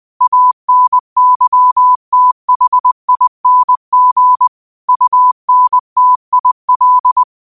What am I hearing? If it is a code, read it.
Morse code, heard as a keyed tone, about 20 wpm, 1000 Hz. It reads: ANYTHING UNTIL
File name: qotdfiles / 20wpm